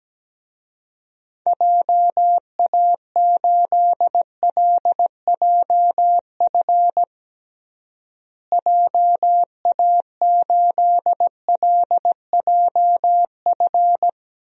Morse code, heard as a keyed tone, about 17 wpm, 700 Hz. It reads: JA8LJF JA8LJF
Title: DXCC-100 Call Signs - 6 Characters - Repeated 3x - 17wpm